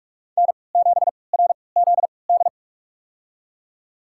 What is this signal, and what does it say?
Morse code, heard as a keyed tone, about 42 wpm, spaced as for 25 wpm, 700 Hz. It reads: N7RZD
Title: USA Call Signs - 5 Characters - 3x Character Spacing - 45wpm